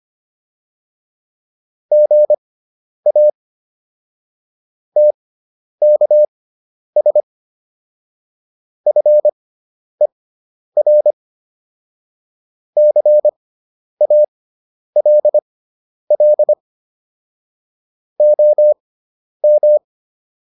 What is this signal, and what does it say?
Morse code, heard as a keyed tone, about 25 wpm, spaced as for 10 wpm, 600 Hz. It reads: GA TKS FER CALL OM